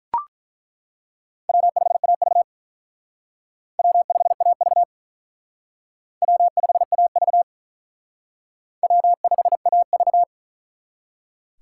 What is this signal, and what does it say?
Morse code, heard as a keyed tone, about 43 wpm, 700 Hz. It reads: W5AV W5AV W5AV W5AV
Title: Speed Racing - USA 1x2 Call Signs 35wpm